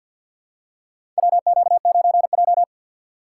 Morse code, heard as a keyed tone, about 53 wpm, 700 Hz. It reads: WQ9J
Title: USA Call Signs - 6x Character Spacing - 50wpm